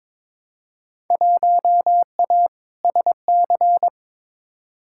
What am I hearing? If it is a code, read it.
Morse code, heard as a keyed tone, about 22 wpm, 700 Hz. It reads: 1A SC
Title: Field Day Exchanges - 22wpm